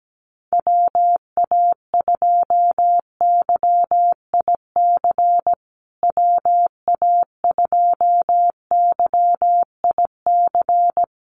Morse code, heard as a keyed tone, about 17 wpm, 700 Hz. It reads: WA2YIC WA2YIC
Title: USA Call Signs Repeated 3x 17wpm